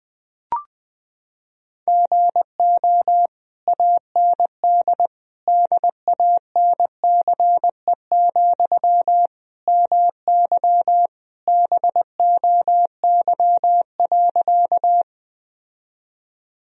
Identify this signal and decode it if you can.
Morse code, heard as a keyed tone, about 20 wpm, 700 Hz. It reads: GO AND DANCE, MY BOY.
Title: Sentences from Top 1000 Words - Review Part 4 - 20wpm